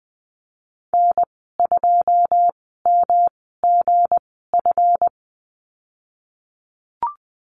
Morse code, heard as a keyed tone, about 20 wpm, spaced as for 15 wpm, 700 Hz. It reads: N2MGF
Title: 161 Lesson 26g - Letter g - Call Signs 20wpm